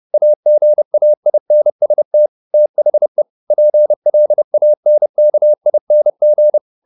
Morse code, heard as a keyed tone, about 30 wpm, 600 Hz. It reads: AGAINST THE PLANKING
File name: the old man and the sea / the old man and the sea 1 30wpm